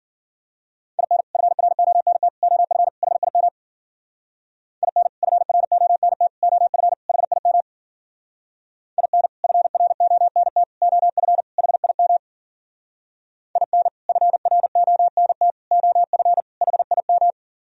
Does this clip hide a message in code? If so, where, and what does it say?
Yes: Morse code from 1.0 to 17.3 s, heard as a keyed tone, about 51 wpm, 700 Hz. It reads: IN FRONT OF HIM IN FRONT OF HIM IN FRONT OF HIM IN FRONT OF HIM